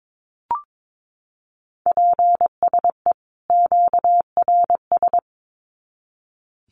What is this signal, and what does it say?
Morse code, heard as a keyed tone, about 22 wpm, 700 Hz. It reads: PSE QRS
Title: Common QSO Phrases 22wpm